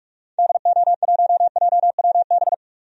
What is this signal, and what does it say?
Morse code, heard as a keyed tone, about 44 wpm, 700 Hz. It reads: DO1JWB
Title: DXCC-100 Call Signs - 6 Characters - 45wpm